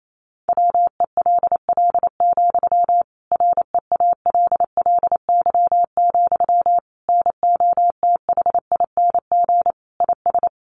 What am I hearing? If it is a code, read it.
Morse code, heard as a keyed tone, about 28 wpm, 700 Hz. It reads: WELL, REALLY, NOTHING IS
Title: Sentences from Top 600 Words 28wpm